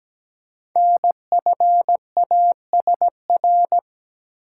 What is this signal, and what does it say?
Morse code, heard as a keyed tone, about 17 wpm, 700 Hz. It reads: NFASR
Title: Code Groups - 5 Letters - 17wpm